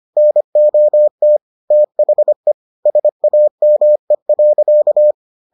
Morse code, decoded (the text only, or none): NOT THE SAME.